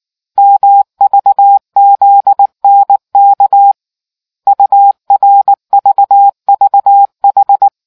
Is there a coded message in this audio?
Yes: Morse code from 0.4 to 7.7 s, heard as a keyed tone, about 19 wpm, 800 Hz. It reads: MVZNK URVVH